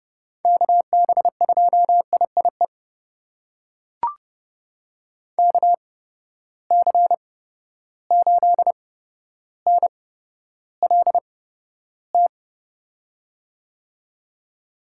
Morse code, heard as a keyed tone, about 31 wpm, spaced as for 13 wpm, 700 Hz. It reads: KB2IIE KC8NLT
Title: USA Call Signs - 6 Characters - 8x Character Spacing - 30wpm